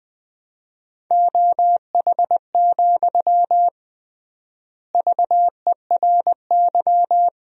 Morse code, heard as a keyed tone, about 20 wpm, 700 Hz. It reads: OH, VERY
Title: 3x Word-Spacing - Sentences from Top 500 Words - 20wpm